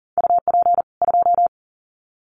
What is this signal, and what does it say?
Morse code, heard as a keyed tone, about 40 wpm, 700 Hz. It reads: UP 2
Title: Common QSO Phrases 40wpm